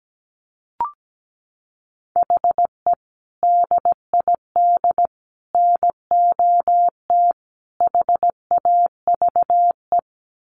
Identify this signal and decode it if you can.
Morse code, heard as a keyed tone, about 17 wpm, 700 Hz. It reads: HE DID NOT HAVE
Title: Sets of 4 Words 17wpm